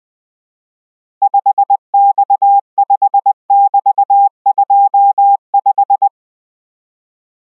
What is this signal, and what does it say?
Morse code, heard as a keyed tone, about 20 wpm, 800 Hz. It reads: 5X5<BT>25